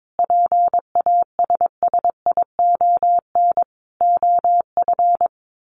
Morse code, heard as a keyed tone, about 22 wpm, 700 Hz. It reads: PASSION OF
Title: Sentences from Top 1400 Words 22wpm